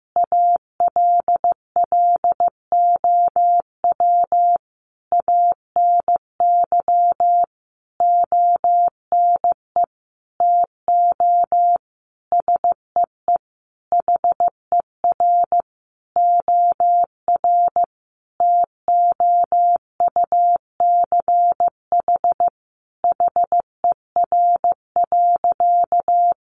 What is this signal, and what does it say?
Morse code, heard as a keyed tone, about 15 wpm, 700 Hz. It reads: ALLOW ANY ONE TO SEE HER OR TOUCH HER.
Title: Sentences from Top 1200 Words 15wpm